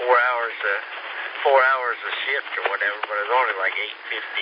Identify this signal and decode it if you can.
Emergency Alert System (SAME) tones, heard as harsh buzzing bursts, with speech. It is heard from 0.5 to 3.9 s.